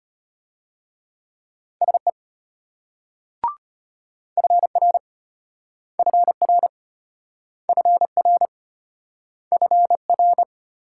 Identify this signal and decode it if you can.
Morse code, heard as a keyed tone, about 33 wpm, 700 Hz. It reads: SE FR FR FR FR